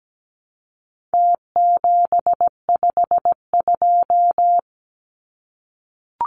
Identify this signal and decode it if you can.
Morse code, heard as a keyed tone, about 17 wpm, 700 Hz. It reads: T752
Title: Numbers - Contest Style - 4-Digit 17wpm